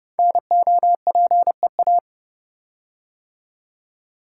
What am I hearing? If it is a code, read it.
Morse code, heard as a keyed tone, about 28 wpm, 700 Hz. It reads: NOPEA
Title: Code Groups - 5 Letters - 30wpm